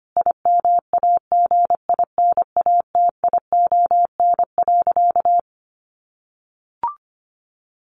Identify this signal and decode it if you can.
Morse code, heard as a keyed tone, about 25 wpm, 700 Hz. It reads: IMAGINATION.
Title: Sentences from Top 1300 Words 25wpm